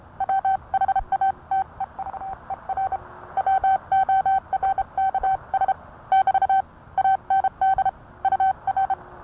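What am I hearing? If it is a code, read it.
Morse code, heard as a keyed tone, about 30 wpm, 770 Hz. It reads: WHATEVER WORKS <BT> AND UR
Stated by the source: Morse code sent by N1EA on a Vibroplex key.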